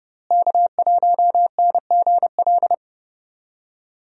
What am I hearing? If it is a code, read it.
Morse code, heard as a keyed tone, about 29 wpm, 700 Hz. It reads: K1NGL
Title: USA Call Signs - 5 Characters - Repeated 3x - 30wpm